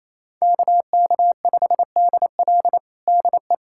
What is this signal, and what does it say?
Morse code, heard as a keyed tone, about 28 wpm, 700 Hz. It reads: KK5DL DE
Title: USA Callsign DE Callsign 28wpm